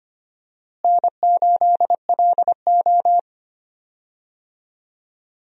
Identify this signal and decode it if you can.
Morse code, heard as a keyed tone, about 25 wpm, 700 Hz. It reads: N8LO
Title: USA Call Signs - 4 Characters - Repeated 2x - 25wpm